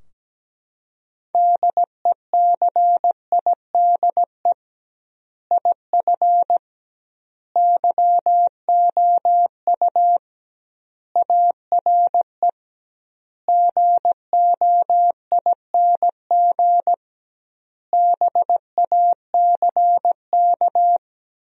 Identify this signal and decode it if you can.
Morse code, heard as a keyed tone, about 17 wpm, 700 Hz. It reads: DECIDE IF YOU ARE GOING BACK
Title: 2x Word-Spacing - Sentences from Top 500 Words - 17wpm